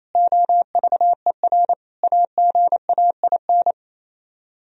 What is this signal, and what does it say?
Morse code, heard as a keyed tone, about 34 wpm, 700 Hz. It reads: OVER AGAIN
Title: Speed-Racing - Sets of 4 Words 28wpm